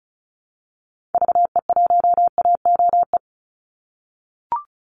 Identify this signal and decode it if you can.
Morse code, heard as a keyed tone, about 34 wpm, 700 Hz. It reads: VE1AOE
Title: Call Signs - 35wpm